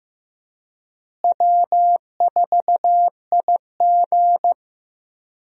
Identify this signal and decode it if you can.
Morse code, heard as a keyed tone, about 15 wpm, 700 Hz. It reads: W4IG
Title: USA Call Signs - 4 Characters - Repeated 3x - 15wpm